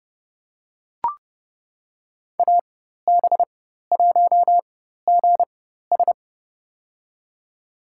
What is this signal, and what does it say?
Morse code, heard as a keyed tone, about 30 wpm, spaced as for 14 wpm, 700 Hz. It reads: AB1GS
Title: USA Call Signs - 5 Characters - 4x Character Spacing - 30wpm